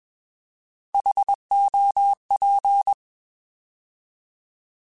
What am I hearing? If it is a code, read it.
Morse code, heard as a keyed tone, about 21 wpm, 800 Hz. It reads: HOP
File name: Alphabetical / Alphabetical mp3s / Hs @20